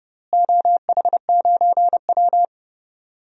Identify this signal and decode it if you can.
Morse code, heard as a keyed tone, about 30 wpm, 700 Hz. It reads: OH9W
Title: DXCC-100 Call Signs - 4 Characters - Repeated 3x - 30wpm